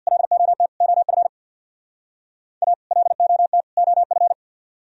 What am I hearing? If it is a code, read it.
Morse code, heard as a keyed tone, about 64 wpm, 700 Hz. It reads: LOT OF A LOT OF